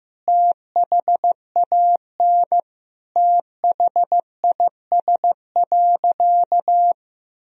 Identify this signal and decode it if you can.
Morse code, heard as a keyed tone, about 15 wpm, 700 Hz. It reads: THAN THIS.
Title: Sentences from Top 1000 Words - Review Part 3 - 15wpm